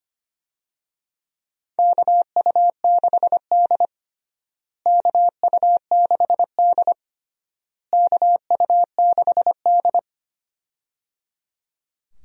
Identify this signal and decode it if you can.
Morse code, heard as a keyed tone, about 25 wpm, 700 Hz. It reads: KU6D KU6D KU6D
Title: USA Call Signs - 4 Characters - Repeated 3x - 25wpm